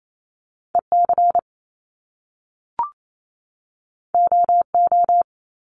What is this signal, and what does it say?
Morse code, heard as a keyed tone, about 28 wpm, 700 Hz. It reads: EC OO
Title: Top Two Letter Combinations in Words 28wpm